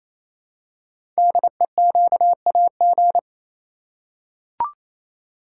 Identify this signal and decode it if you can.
Morse code, heard as a keyed tone, about 27 wpm, 700 Hz. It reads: DEQAG
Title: Code Groups - 5 Letters - 28wpm